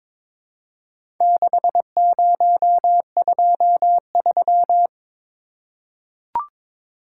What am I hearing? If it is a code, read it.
Morse code, heard as a keyed tone, about 22 wpm, 700 Hz. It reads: 6023
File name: practice / 22wpm / Four Letter-Number Combinations - 22wpm